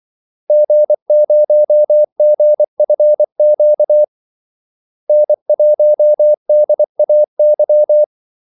Morse code, heard as a keyed tone, about 24 wpm, 600 Hz. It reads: G0GFQ N1DAY
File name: Callsigns-24wpm